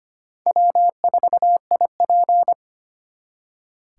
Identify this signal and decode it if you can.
Morse code, heard as a keyed tone, about 32 wpm, 700 Hz. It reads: W4IP